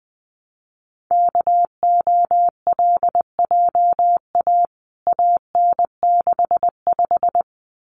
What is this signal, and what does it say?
Morse code, heard as a keyed tone, about 20 wpm, 700 Hz. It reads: KOLJA AN65
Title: CWT Exchange 20wpm